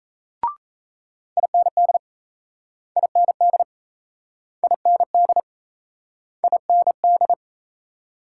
Morse code, heard as a keyed tone, about 38 wpm, 700 Hz. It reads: IND IND IND IND